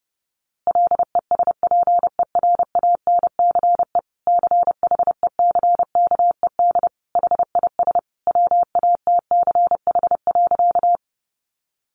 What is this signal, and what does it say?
Morse code, heard as a keyed tone, about 30 wpm, 700 Hz. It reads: LESPERANCE CHECKED HIS WATCH.
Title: A Sound Of Thunder 30wpm